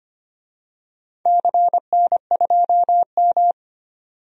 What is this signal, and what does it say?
Morse code, heard as a keyed tone, about 25 wpm, 700 Hz. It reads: CN2M
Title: DXCC-100 Call Signs - 4 Characters - 25wpm